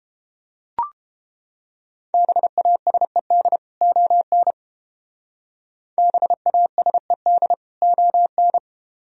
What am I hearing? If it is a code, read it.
Morse code, heard as a keyed tone, about 30 wpm, 700 Hz. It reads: BASED ON BASED ON